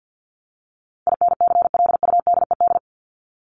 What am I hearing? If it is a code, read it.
Morse code, heard as a keyed tone, about 50 wpm, 700 Hz. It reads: INCLUDED
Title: ED Suffix - Top 100 Words with Base Words 50wpm